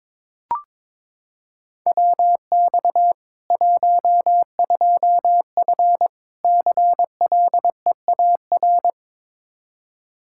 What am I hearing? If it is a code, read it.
Morse code, heard as a keyed tone, about 21 wpm, 700 Hz. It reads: WX 12F CLEAR
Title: Practice QSO - Exchange 2 - Info - 22wpm